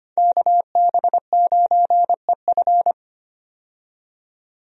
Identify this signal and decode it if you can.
Morse code, heard as a keyed tone, about 25 wpm, 700 Hz. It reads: KB9EF
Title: USA Call Signs - 5 Characters - 25wpm